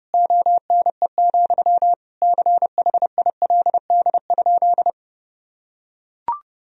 Morse code, heard as a keyed tone, about 30 wpm, 700 Hz. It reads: ONE, CHILD?
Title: Sentences from Top 1000 Words - Review Part 1 - 30wpm